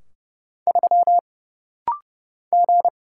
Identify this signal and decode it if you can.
Morse code, heard as a keyed tone, about 30 wpm, 700 Hz. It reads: 3 G
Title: Single Letter-Number - Rapid-Fire 30wpm